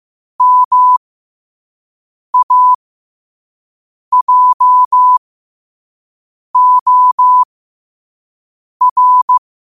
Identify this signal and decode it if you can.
Morse code, heard as a keyed tone, about 15 wpm, spaced as for 5 wpm, 1000 Hz. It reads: MAJOR
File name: qotdfiles / 5wpm